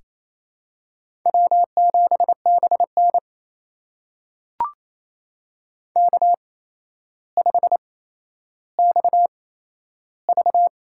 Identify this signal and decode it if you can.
Morse code, heard as a keyed tone, about 27 wpm, spaced as for 12 wpm, 700 Hz. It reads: W7BN K5XV
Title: USA Call Signs - 8x Character Spacing - 28wpm